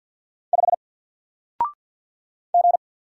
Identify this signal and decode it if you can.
Morse code, heard as a keyed tone, about 49 wpm, 700 Hz. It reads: 5 G
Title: Single Letter-Number Alt 50wpm